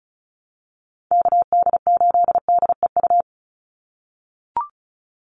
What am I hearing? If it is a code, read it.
Morse code, heard as a keyed tone, about 34 wpm, 700 Hz. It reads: KD8DEU
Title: NA SOTA Chaser Callsigns 35wpm